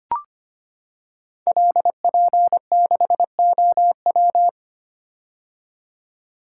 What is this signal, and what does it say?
Morse code, heard as a keyed tone, about 25 wpm, 700 Hz. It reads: LP6OW
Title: Code Groups - 5 Letter:Numbers - 25wpm